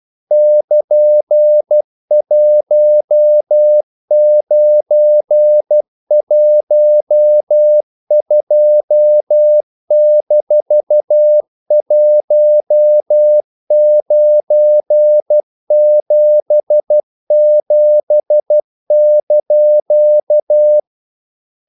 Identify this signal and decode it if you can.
Morse code, heard as a keyed tone, about 12 wpm, 600 Hz. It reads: <KN>1912-1977)
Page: English quotations in CW (telegraphy - morse)